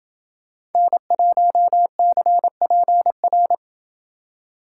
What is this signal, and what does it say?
Morse code, heard as a keyed tone, about 26 wpm, 700 Hz. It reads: N1CPR